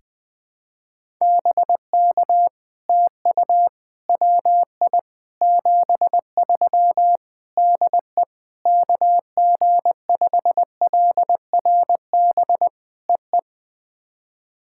Morse code, heard as a keyed tone, about 20 wpm, 700 Hz. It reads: BK TU WI 73 DE KG5LRB EE
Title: POTA QSO Practice - 20wpm